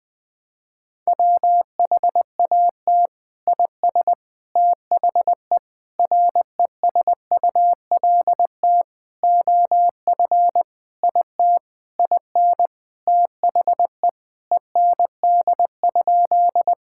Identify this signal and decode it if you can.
Morse code, heard as a keyed tone, about 21 wpm, 700 Hz. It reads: WHAT IS THE RESULT OF IT IN THE END?
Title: Sentences from Top 1000 Words - Review Part 5 - 20wpm